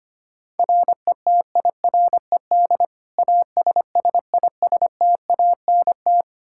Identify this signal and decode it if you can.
Morse code, heard as a keyed tone, about 25 wpm, 700 Hz. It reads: RETIRED ASSISTANT
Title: Practice QSO - Exchange 3 - More Info - 25wpm